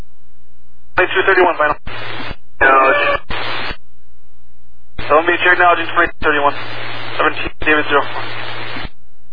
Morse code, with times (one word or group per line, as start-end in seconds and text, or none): none